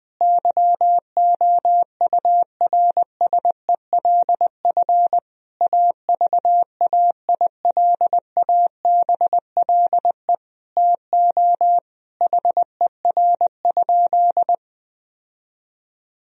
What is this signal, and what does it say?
Morse code, heard as a keyed tone, about 21 wpm, 700 Hz. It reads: YOURSELF AVAILABLE TO HER?
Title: Sentences from Top 1000 Words - Review Part 3 - 20wpm